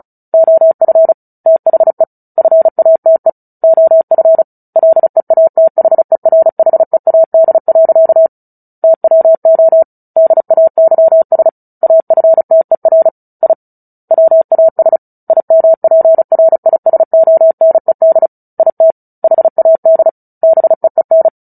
Morse code, heard as a keyed tone, about 34 wpm, 650 Hz. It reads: OF THE FATE OF LEATHERHEAD. TWO DAYS AFTER I WAS IMPRISONED IT HAD BEEN
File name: Book Two / 09-Chapter 9 - 35 WPM